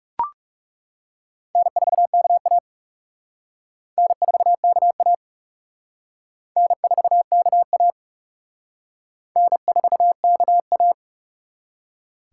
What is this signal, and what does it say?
Morse code, heard as a keyed tone, about 38 wpm, 700 Hz. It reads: N4KA N4KA N4KA N4KA
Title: Speed Racing - USA Call Signs 30wpm